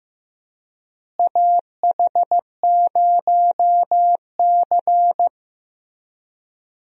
Morse code, heard as a keyed tone, about 16 wpm, 700 Hz. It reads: AH0C
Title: USA Call Signs - 4 Characters - Repeated 2x - 15wpm